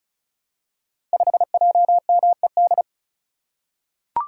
Morse code, heard as a keyed tone, about 34 wpm, 700 Hz. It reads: 5JMED
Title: Code Groups - 5 Letter:Numbers - 35wpm